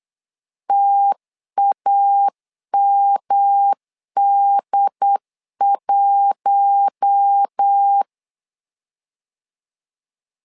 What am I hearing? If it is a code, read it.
Morse code, heard as a keyed tone, about 8 wpm, 800 Hz. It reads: TAMD1
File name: LEZIONI / 4 Q1 8wpm 3035